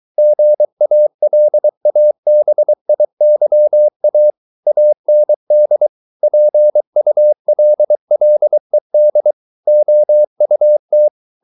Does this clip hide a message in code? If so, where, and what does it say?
Yes: Morse code from 0.2 to 11.1 s, heard as a keyed tone, about 23 wpm, 600 Hz. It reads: GALABIYA AND PULLED OUT